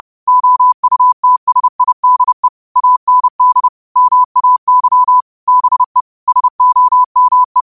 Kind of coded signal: Morse code